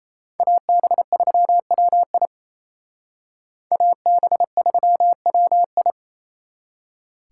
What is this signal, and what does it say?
Morse code, heard as a keyed tone, about 36 wpm, 700 Hz. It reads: AB3WI AB3WI